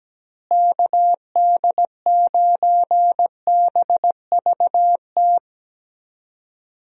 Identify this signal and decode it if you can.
Morse code, heard as a keyed tone, about 17 wpm, 700 Hz. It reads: KD9BVT
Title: USA Call Signs - 6 Characters - Repeated 2x - 17wpm